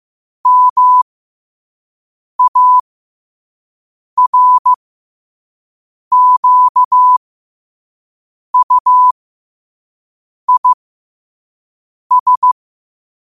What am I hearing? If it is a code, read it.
Morse code, heard as a keyed tone, about 15 wpm, spaced as for 5 wpm, 1000 Hz. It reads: MARQUIS
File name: qotdfiles / 5wpm